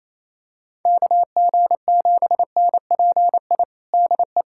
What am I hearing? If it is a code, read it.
Morse code, heard as a keyed tone, about 28 wpm, 700 Hz. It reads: KG7NPI DE
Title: USA Callsign DE Callsign - 6 Characters - 28wpm